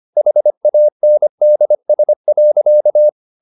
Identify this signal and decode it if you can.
Morse code, heard as a keyed tone, about 25 wpm, 600 Hz. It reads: HANDS.